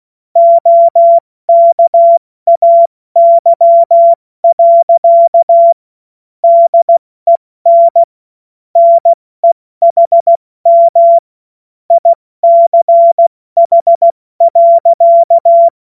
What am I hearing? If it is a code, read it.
Morse code, heard as a keyed tone, about 16 wpm, 680 Hz. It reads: OKAY. DEN NEHM ICH.
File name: Morsekurs / Witze / Witz 05 März 2024 70 F80